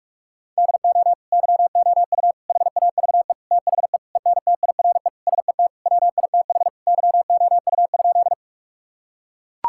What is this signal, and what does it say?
Morse code, heard as a keyed tone, about 41 wpm, 700 Hz. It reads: DO YOU HAVE THE ENTIRE SET WITH YOU?
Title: Sentences from Top 1000 Words - Review Part 5 - 45wpm